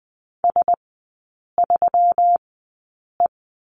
Morse code, heard as a keyed tone, about 20 wpm, spaced as for 8 wpm, 700 Hz. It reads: S3E
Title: 198 Lesson 32e - Letter k - Three Characters 20wpm